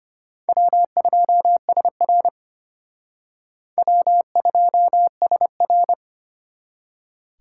Morse code, heard as a keyed tone, about 32 wpm, 700 Hz. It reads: W2SR W2SR